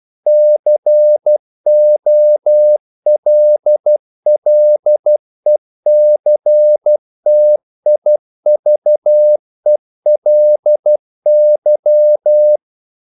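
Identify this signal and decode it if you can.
Morse code, heard as a keyed tone, about 12 wpm, 600 Hz. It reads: COLLECTIVELY
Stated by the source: Morse practice sending English quotations.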